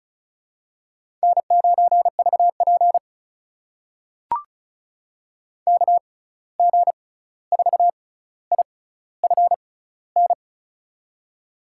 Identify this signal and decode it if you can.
Morse code, heard as a keyed tone, about 36 wpm, spaced as for 16 wpm, 700 Hz. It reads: N9VP KG4IFN
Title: USA Call Signs - 6x Character Spacing - 35wpm